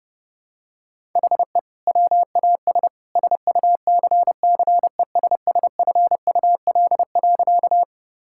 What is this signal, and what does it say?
Morse code, heard as a keyed tone, about 30 wpm, 700 Hz. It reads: HE WAS SUCCESSFUL.
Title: Sentences from Top 1000 Words - Review Part 6 - 30wpm